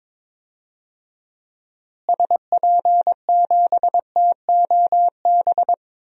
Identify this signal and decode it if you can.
Morse code, heard as a keyed tone, about 22 wpm, 700 Hz. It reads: SP7TOB
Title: DXCC-100 Call Signs - 6 Characters - Repeated 3x - 22wpm